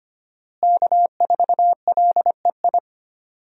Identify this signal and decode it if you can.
Morse code, heard as a keyed tone, about 25 wpm, 700 Hz. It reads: K4LEI
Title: USA Call Signs - 5 Characters - Repeated 3x - 25wpm